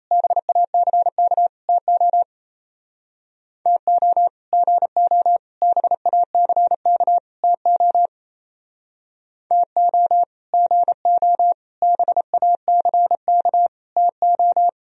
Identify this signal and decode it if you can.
Morse code, heard as a keyed tone, about 35 wpm, 700 Hz. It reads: BACK TO TO GO BACK TO TO GO BACK TO
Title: Speed-Racing - Sets of 4 Words 28wpm